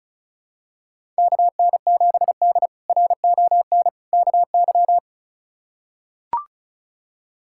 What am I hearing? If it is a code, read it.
Morse code, heard as a keyed tone, about 34 wpm, 700 Hz. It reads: KN7D RON KY
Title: K1USN SST QSO Practice - 35wpm